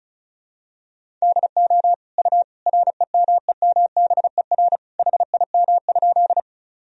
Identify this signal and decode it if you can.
Morse code, heard as a keyed tone, about 36 wpm, 700 Hz. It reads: DO U REMEMBER HIM?
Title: Sentences from Top 400 Words and Abbreviations 35wpm